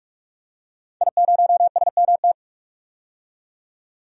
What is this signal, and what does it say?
Morse code, heard as a keyed tone, about 46 wpm, 700 Hz. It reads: I0SMT
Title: DXCC-100 Call Signs - 5 Characters - 45wpm